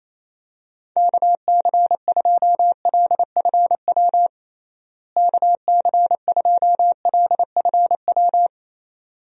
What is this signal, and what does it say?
Morse code, heard as a keyed tone, about 28 wpm, 700 Hz. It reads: KC2LFW KC2LFW
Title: USA Call Signs - 6 Characters - Repeated 2x - 28wpm